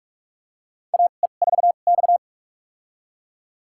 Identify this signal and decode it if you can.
Morse code, heard as a keyed tone, about 41 wpm, spaced as for 31 wpm, 700 Hz. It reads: AE4X
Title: USA Call Signs - 4 Characters - 2x Character Spacing - 45wpm